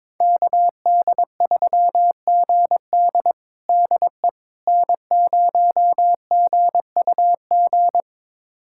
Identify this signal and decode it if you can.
Morse code, heard as a keyed tone, about 22 wpm, 700 Hz. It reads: KD3GD DE N0GUG
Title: USA Callsign DE Callsign - 5 Characters - 22wpm